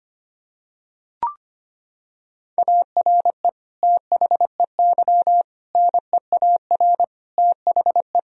Morse code, heard as a keyed tone, about 25 wpm, 700 Hz. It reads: ARE THEY NEAR THE
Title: Sentences from Top 1000 Words - Review Part 1 - 25wpm